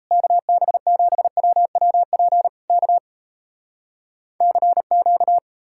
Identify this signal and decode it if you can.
Morse code, heard as a keyed tone, about 40 wpm, 700 Hz. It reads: KB7WWP K CQ